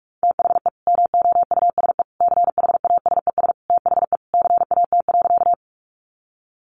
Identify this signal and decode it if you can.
Morse code, heard as a keyed tone, about 44 wpm, 700 Hz. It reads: THE MOUSE CHASES THE CAT.
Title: Easy Phrases 45wpm